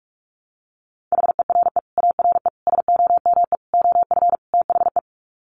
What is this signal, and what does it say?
Morse code, heard as a keyed tone, about 44 wpm, 700 Hz. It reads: HERE ARE SOME OF THE
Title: Sets of 5 Words Encore 45wpm